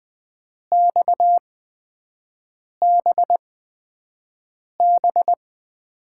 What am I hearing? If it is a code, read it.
Morse code, heard as a keyed tone, about 20 wpm, spaced as for 5 wpm, 700 Hz. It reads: XBB
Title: USA Call Signs - 5 Characters - 8x Character Spacing - 20wpm